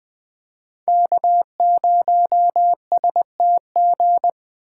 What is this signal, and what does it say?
Morse code, heard as a keyed tone, about 20 wpm, 700 Hz. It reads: K0STG
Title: USA Call Signs - 8x Character Spacing - 20wpm